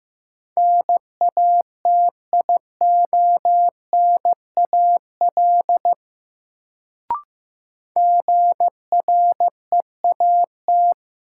Morse code, heard as a keyed tone, about 15 wpm, 700 Hz. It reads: NATIONAL GREAT
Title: Sets of 2 Words - New Word Drill - 15wpm